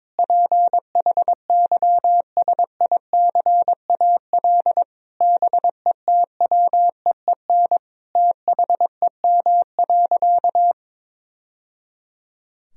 Morse code, heard as a keyed tone, about 22 wpm, 700 Hz. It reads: PHYSICAL BETWEEN THEM.
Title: Sentences from Top 1000 Words - Review Part 4 - 22wpm